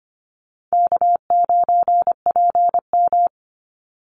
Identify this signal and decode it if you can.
Morse code, heard as a keyed tone, about 25 wpm, 690 Hz. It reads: K9PM
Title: NA SOTA Activator Callsigns 25wpm